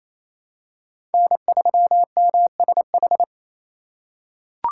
Morse code, heard as a keyed tone, about 28 wpm, 700 Hz. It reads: N3MSH
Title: USA Call Signs - 3x Character Spacing - 28wpm